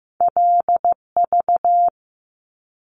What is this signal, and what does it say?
Morse code, heard as a keyed tone, about 15 wpm, 700 Hz. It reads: LV
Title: Top Two Letter Combinations in Words 15wpm